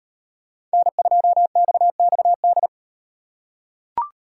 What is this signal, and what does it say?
Morse code, heard as a keyed tone, about 38 wpm, 700 Hz. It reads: N2XXD